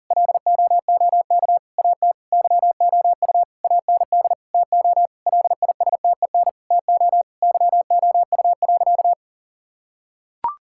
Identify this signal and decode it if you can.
Morse code, heard as a keyed tone, about 37 wpm, 700 Hz. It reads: LOOK AT YOU AND TO LISTEN TO YOU.
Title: Sentences from Top 1600 Words 40wpm